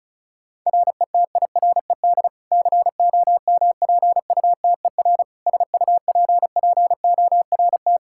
Morse code, heard as a keyed tone, about 34 wpm, 700 Hz. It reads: RETIRED COMPUTER SUPPORT
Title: Practice QSO - Exchange 3 - More Info - 35wpm